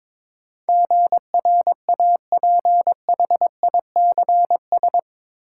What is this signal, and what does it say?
Morse code, heard as a keyed tone, about 22 wpm, 700 Hz. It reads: GRAPHICS
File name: practice / 22wpm / Eight Letter Words - 22wpm